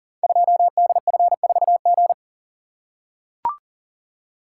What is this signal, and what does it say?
Morse code, heard as a keyed tone, about 40 wpm, 700 Hz. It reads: 2DF4G
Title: Code Groups - 5 Letter:Numbers - 40wpm